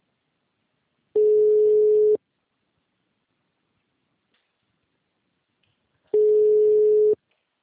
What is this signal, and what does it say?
Ringback tone (Europe, 425 Hz).
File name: out- -2020- - - .18582